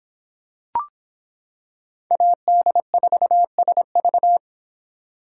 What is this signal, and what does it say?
Morse code, heard as a keyed tone, about 26 wpm, 700 Hz. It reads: AD4SV